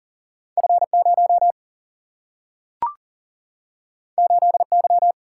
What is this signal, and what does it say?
Morse code, heard as a keyed tone, about 39 wpm, 700 Hz. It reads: F0 8Y
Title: Single-Letter Single-Number 40wpm